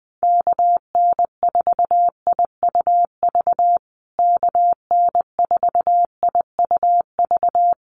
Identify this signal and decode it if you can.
Morse code, heard as a keyed tone, about 20 wpm, 700 Hz. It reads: KN4IUV KN4IUV
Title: USA Call Signs Repeated 3x 20wpm